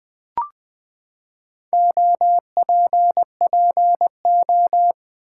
Morse code, heard as a keyed tone, about 20 wpm, 700 Hz. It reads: OPPO
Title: Build A Word - 4 Letters At A Time - 10-Letter Words - 20wpm